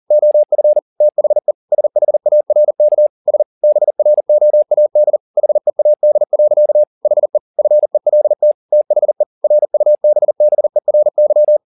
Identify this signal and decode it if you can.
Morse code, heard as a keyed tone, about 40 wpm, 600 Hz. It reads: OF THE SHARK S BROAD HEAD. HE FELT THE RUBBERY
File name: the old man and the sea 2 40wpm